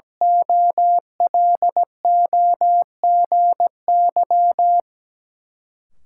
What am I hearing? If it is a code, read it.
Morse code, heard as a keyed tone, about 17 wpm, 700 Hz. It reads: OLOGY
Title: Sentences from Top 1400 Words - New Word Drill Alt - 17wpm